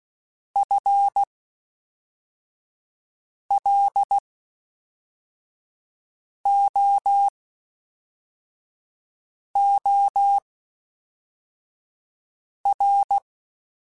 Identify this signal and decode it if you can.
Morse code, heard as a keyed tone, about 16 wpm, spaced as for 4 wpm, 800 Hz. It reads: FLOOR
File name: Spaced / FL Runs sp mp3s / FL Runs sp @5